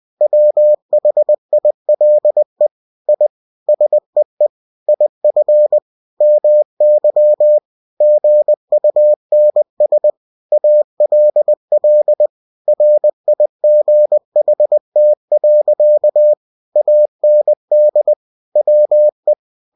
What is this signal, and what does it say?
Morse code, heard as a keyed tone, about 20 wpm, 600 Hz. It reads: WHILE I SEE IF MY GUNS ALL RIGHT. AND WE
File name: cw-books / winnie-the-pooh / winnie-20wpm-page-0077